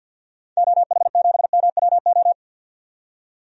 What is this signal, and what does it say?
Morse code, heard as a keyed tone, about 52 wpm, 700 Hz. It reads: OH7MWO
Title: DXCC-100 Call Signs - 6 Characters - 50wpm